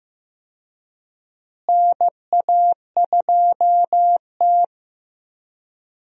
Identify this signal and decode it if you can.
Morse code, heard as a keyed tone, about 15 wpm, 700 Hz. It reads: NA2T
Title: USA Call Signs - 4 Characters - Repeated 2x - 15wpm